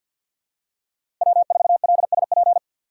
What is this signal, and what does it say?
Morse code, heard as a keyed tone, about 51 wpm, 700 Hz. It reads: W4LSP
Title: USA Call Signs - 5 Characters - 2x Character Spacing - 50wpm